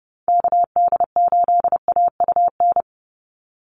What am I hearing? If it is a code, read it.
Morse code, heard as a keyed tone, about 30 wpm, 700 Hz. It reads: KD8AUN
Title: USA Call Signs 30wpm